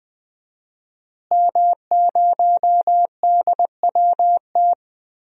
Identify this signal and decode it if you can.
Morse code, heard as a keyed tone, about 20 wpm, 700 Hz. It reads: M0DWT
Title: DXCC-100 Call Signs - 5 Characters - Repeated 3x - 20wpm